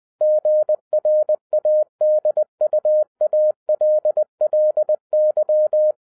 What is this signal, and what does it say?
Morse code, heard as a keyed tone, about 20 wpm, 610 Hz. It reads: GRADUALLY